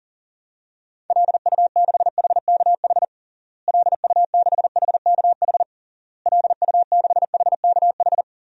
Morse code, heard as a keyed tone, about 40 wpm, 700 Hz. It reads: LU6HKH LU6HKH LU6HKH
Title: DXCC-100 Call Signs - 6 Characters - Repeated 3x - 40wpm